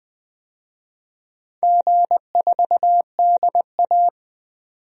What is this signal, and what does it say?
Morse code, heard as a keyed tone, about 20 wpm, 700 Hz. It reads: G4DA
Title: DXCC-100 Call Signs - 4 Characters - 20wpm